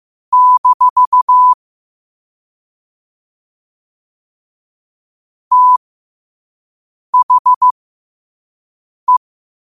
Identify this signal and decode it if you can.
Morse code, heard as a keyed tone, about 15 wpm, spaced as for 5 wpm, 1000 Hz. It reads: - THE
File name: qotdfiles / 5wpm